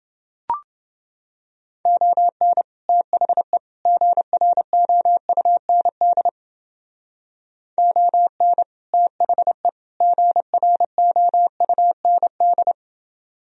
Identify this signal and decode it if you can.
Morse code, heard as a keyed tone, about 29 wpm, 700 Hz. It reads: ON THE GROUND ON THE GROUND